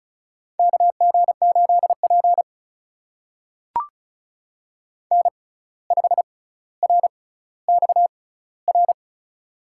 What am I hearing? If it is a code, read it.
Morse code, heard as a keyed tone, about 36 wpm, spaced as for 17 wpm, 700 Hz. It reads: KG8P N5RXR
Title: USA Call Signs - 6x Character Spacing - 35wpm